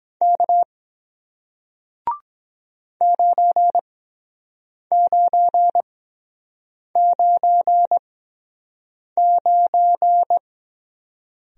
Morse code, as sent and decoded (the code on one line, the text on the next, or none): -.- / ----. ----. ----. ----.
K 9999